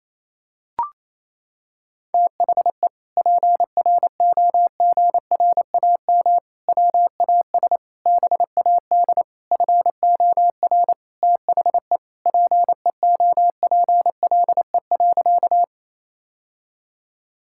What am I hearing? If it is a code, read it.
Morse code, heard as a keyed tone, about 28 wpm, 700 Hz. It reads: THE PROGRAM WAS BAD FOR THE PEOPLE.
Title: Sentences from Top 1000 Words - Review Part 5 - 28wpm